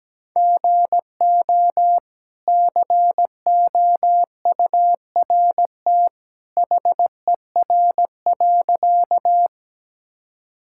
Morse code, heard as a keyed tone, about 17 wpm, 700 Hz. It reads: GO COURT HER.
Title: Sentences from Top 1600 Words 17wpm